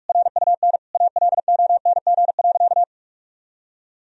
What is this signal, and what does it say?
Morse code, heard as a keyed tone, about 46 wpm, 700 Hz. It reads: RUN ALONG.
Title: Sentences from Top 1000 Words - Review Part 1 - 45wpm